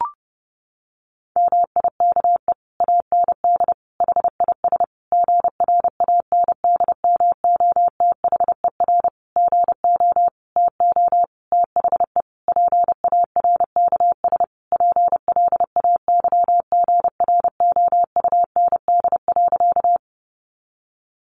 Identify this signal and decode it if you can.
Morse code, heard as a keyed tone, about 30 wpm, 700 Hz. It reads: MIKE AND HIS GRANDMOTHER GO TO THE PARKS PLAYGROUND.
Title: Additional Easy Phrases 30wpm